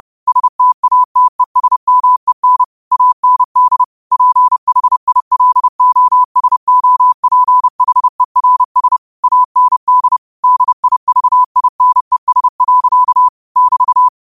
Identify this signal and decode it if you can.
Morse code, heard as a keyed tone, about 30 wpm, 1000 Hz. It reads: STATESMEN AND PHILOSOPHERS AND DIVINES. <BT>
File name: qotdfiles / 30wpm